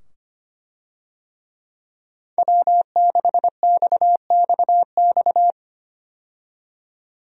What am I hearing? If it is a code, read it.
Morse code, heard as a keyed tone, about 25 wpm, 700 Hz. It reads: W6XXX
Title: USA Call Signs - 5 Characters - 25wpm